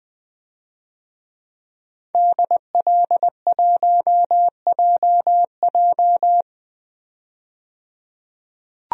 Morse code, heard as a keyed tone, about 20 wpm, 700 Hz. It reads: DL1JJ
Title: DXCC-100 Call Signs - 5 Characters - 20wpm